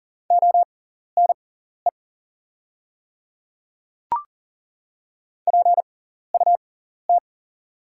Morse code, heard as a keyed tone, about 40 wpm, spaced as for 14 wpm, 700 Hz. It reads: ONE PUT
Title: 6x Character Spacing - Top 200 Words - 40wpm